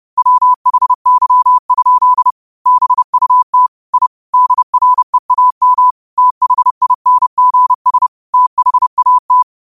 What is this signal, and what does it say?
Morse code, heard as a keyed tone, about 30 wpm, 1000 Hz. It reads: WHY? BUT I DREAM THINGS THAT